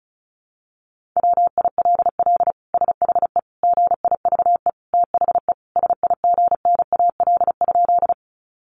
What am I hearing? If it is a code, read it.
Morse code, heard as a keyed tone, about 34 wpm, 700 Hz. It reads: WILL SHE GIVE THE SIGNAL?
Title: Sentences from Top 1400 Words 35wpm